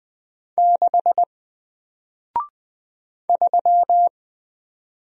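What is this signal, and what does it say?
Morse code, heard as a keyed tone, about 20 wpm, 700 Hz. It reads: 6 3
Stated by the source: Single letter and number Morse drill at 20 wpm, from N4LCD.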